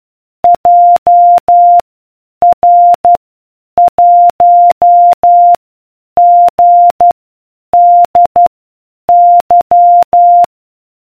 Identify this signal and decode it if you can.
Morse code, heard as a keyed tone, about 12 wpm, spaced as for 8 wpm, 700 Hz. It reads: JR1GDY